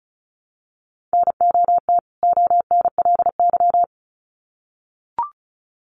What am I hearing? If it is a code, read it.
Morse code, heard as a keyed tone, about 34 wpm, 700 Hz. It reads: NOT ONLY
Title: Sets of 2 Words 35wpm